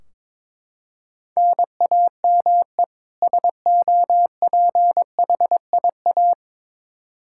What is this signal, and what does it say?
Morse code, heard as a keyed tone, about 22 wpm, 700 Hz. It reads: NAME SOPHIA
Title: Practice QSO - Exchange 1 - Beginning - 22wpm